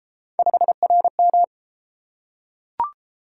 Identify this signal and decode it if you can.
Morse code, heard as a keyed tone, about 29 wpm, 700 Hz. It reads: 5RM